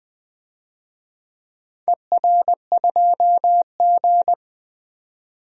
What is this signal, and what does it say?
Morse code, heard as a keyed tone, about 20 wpm, 700 Hz. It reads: ER2G
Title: DXCC-100 Call Signs - 4 Characters - 20wpm